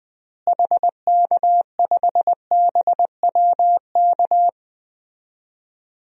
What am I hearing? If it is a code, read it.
Morse code, heard as a keyed tone, about 20 wpm, 700 Hz. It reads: HK5BWK
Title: DXCC-100 Call Signs - 6 Characters - Repeated 3x - 20wpm